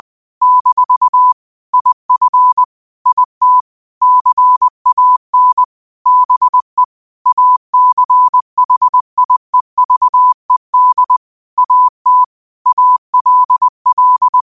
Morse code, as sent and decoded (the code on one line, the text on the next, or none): -....- / .. ..-. / .. - / -.-. .- -. / -... / .- -.-. .... .. ...- . -.. / .- - / .- .-.. .-..
- IF IT CAN BE ACHIEVED AT ALL